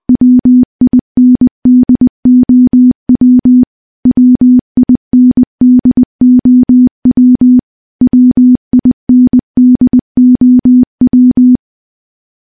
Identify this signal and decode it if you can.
Morse code, heard as a keyed tone, about 20 wpm, 250 Hz. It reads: WINDOW WINDOW WINDOW
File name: morse-training-20WPM